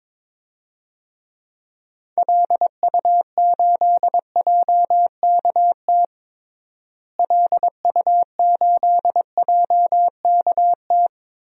Morse code, heard as a keyed tone, about 22 wpm, 700 Hz. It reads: LU8JKT LU8JKT
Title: DXCC-100 Call Signs - 6 Characters - Repeated 3x - 22wpm